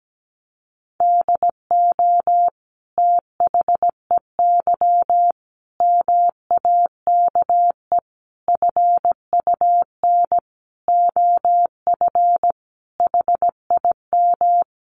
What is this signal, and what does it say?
Morse code, heard as a keyed tone, about 17 wpm, 700 Hz. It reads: DO THEY MAKE FUN OF HIM
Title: Sentences from Top 1300 Words 17wpm